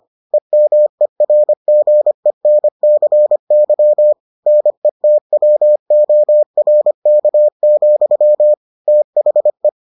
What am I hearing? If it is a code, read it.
Morse code, heard as a keyed tone, about 25 wpm, 600 Hz. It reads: EMERGENCY NETWORK, THE